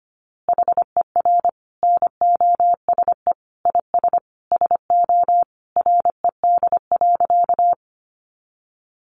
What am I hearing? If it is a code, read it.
Morse code, heard as a keyed tone, about 25 wpm, 700 Hz. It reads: HER NOSE IS SO RED.
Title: Easy Phrases 25wpm